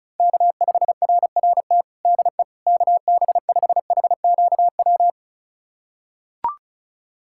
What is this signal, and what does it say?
Morse code, heard as a keyed tone, about 34 wpm, 700 Hz. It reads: K5RRT DE KB5HQW
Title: USA Callsign DE Callsign 35wpm